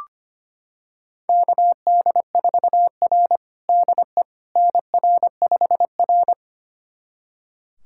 Morse code, heard as a keyed tone, about 25 wpm, 700 Hz. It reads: KD4R DE NR5R
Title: USA Callsign DE Callsign - 4 Characters - 25wpm